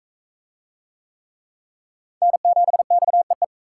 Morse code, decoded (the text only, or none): N7XEE